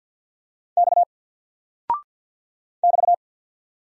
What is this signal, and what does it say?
Morse code, heard as a keyed tone, about 49 wpm, 700 Hz. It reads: X <BT>
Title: Single Character (Letter:Number:Punctuation:Prosign) 50wpm